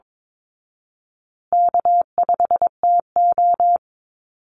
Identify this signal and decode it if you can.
Morse code, heard as a keyed tone, about 22 wpm, 700 Hz. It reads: K5TO
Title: Call Signs Repeated 3x - 22wpm